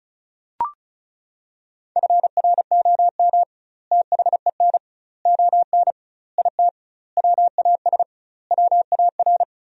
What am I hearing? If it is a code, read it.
Morse code, heard as a keyed tone, about 36 wpm, 700 Hz. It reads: FROM THEN ON IT WAS WAR
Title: 2x Word-Spacing - Sentences from Top 400 Words - 35wpm